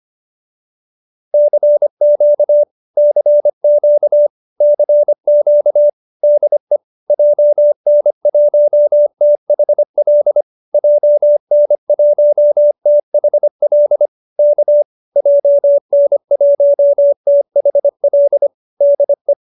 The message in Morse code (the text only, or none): CQ CQ CQ DE JN1THL JN1THL K JN1THL DE